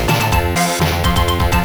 There is no code.